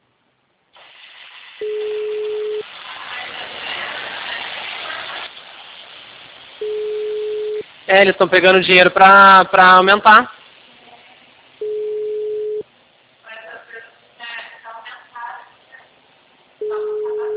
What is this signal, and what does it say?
Ringback tone (Europe, 425 Hz), with speech.